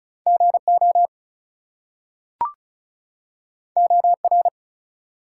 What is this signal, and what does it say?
Morse code, heard as a keyed tone, about 36 wpm, 700 Hz. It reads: GO OR
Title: Two Letter Words - 35wpm